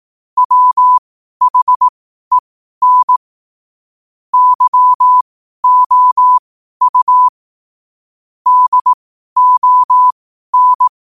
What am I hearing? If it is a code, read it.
Morse code, heard as a keyed tone, about 18 wpm, spaced as for 12 wpm, 1000 Hz. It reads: WHEN YOU DON
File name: qotdfiles / 12wpm